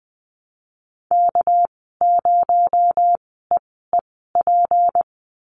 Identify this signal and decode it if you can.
Morse code, heard as a keyed tone, about 20 wpm, spaced as for 15 wpm, 700 Hz. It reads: K0EEP
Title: 219 Lesson 35g - Number 0 - Call Signs 20wpm